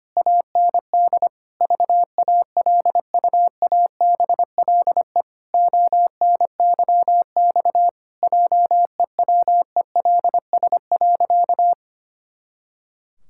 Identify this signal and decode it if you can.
Morse code, heard as a keyed tone, about 25 wpm, 700 Hz. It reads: AND VALUABLE ONYX JEWELS.
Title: Pangram Challenge v2 - 25wpm